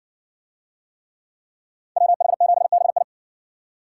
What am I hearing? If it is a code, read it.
Morse code, heard as a keyed tone, about 57 wpm, 700 Hz. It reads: WH7BI